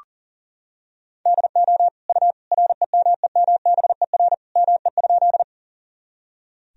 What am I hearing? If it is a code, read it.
Morse code, heard as a keyed tone, about 40 wpm, 700 Hz. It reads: DO U REMEMBER ME?
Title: Sentences from Top 400 Words and Abbreviations 40wpm